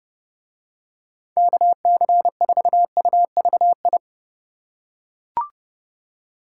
Morse code, heard as a keyed tone, about 29 wpm, 700 Hz. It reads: KC4UVI